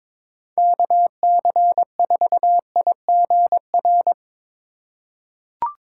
Morse code, heard as a keyed tone, about 22 wpm, 700 Hz. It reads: KC4IGR
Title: USA Call Signs - 6 Characters - 2x Character Spacing - 22wpm